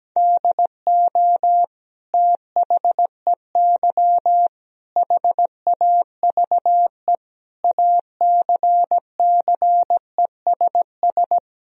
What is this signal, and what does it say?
Morse code, heard as a keyed tone, about 17 wpm, 700 Hz. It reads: DO THEY HAVE ACCESS
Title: Sentences from Top 1000 Words - Review Part 4 - 17wpm